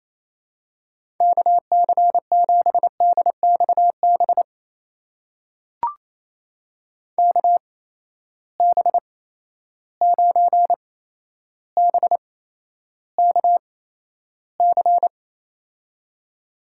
Morse code, heard as a keyed tone, about 28 wpm, spaced as for 12 wpm, 700 Hz. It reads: KC7DXB KB9BKC
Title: USA Call Signs - 6 Characters - 8x Character Spacing - 28wpm